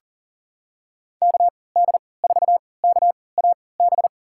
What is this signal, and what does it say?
Morse code, heard as a keyed tone, about 40 wpm, spaced as for 23 wpm, 700 Hz. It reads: KD4KAB
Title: USA Call Signs - 6 Characters - 3x Character Spacing - 40wpm